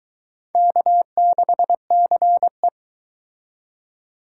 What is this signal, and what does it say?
Morse code, heard as a keyed tone, about 23 wpm, 700 Hz. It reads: K6CE